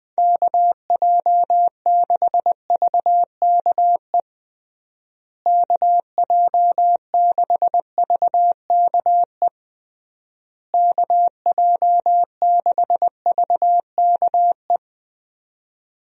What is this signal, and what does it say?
Morse code, heard as a keyed tone, about 20 wpm, 700 Hz. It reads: KJ6VKE KJ6VKE KJ6VKE
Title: USA Call Signs - 6 Characters - Repeated 3x - 20wpm